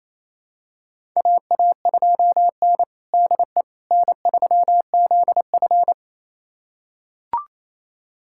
Morse code, heard as a keyed tone, about 28 wpm, 700 Hz. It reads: AA2N DE N3ZF
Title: USA Callsign DE Callsign - 4 Characters - 28wpm